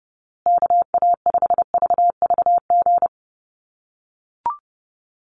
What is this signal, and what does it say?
Morse code, heard as a keyed tone, about 30 wpm, 700 Hz. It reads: KA5VVG
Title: USA Call Signs Repeated 3x 30wpm